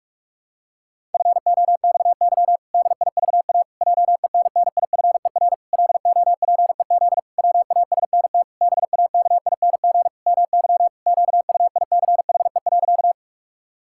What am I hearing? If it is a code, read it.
Morse code, heard as a keyed tone, about 41 wpm, 700 Hz. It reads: FOXY DIVA JENNIFER LOPEZ WASNT BAKING MY QUICHE.
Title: Pangram Challenge v2 - 45wpm